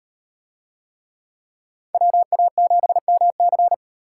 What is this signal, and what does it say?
Morse code, heard as a keyed tone, about 38 wpm, 700 Hz. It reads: WA7MC